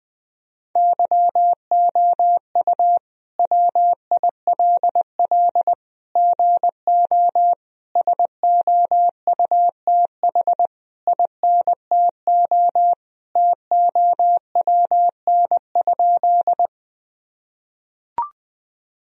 Morse code, heard as a keyed tone, about 21 wpm, 700 Hz. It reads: YOU WILL GO SOUTH INTO TOWN?
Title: Sentences from Top 1000 Words - Review Part 2 - 20wpm